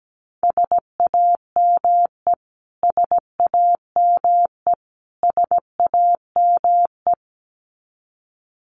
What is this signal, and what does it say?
Morse code, heard as a keyed tone, about 17 wpm, 700 Hz. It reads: SAME SAME SAME
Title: Top 300 Words Repeated 17wpm